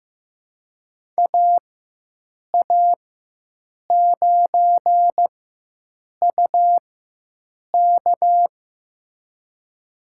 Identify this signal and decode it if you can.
Morse code, heard as a keyed tone, about 15 wpm, spaced as for 7 wpm, 700 Hz. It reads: AA9UK
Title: USA Call Signs - 5 Characters - 4x Character Spacing - 15wpm